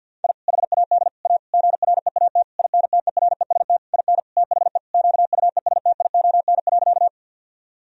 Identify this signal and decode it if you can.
Morse code, heard as a keyed tone, about 51 wpm, 700 Hz. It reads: I HAD A GREAT INTEREST IN THE QUESTION.
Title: Sentences from Top 1000 Words - Review Part 2 - 50wpm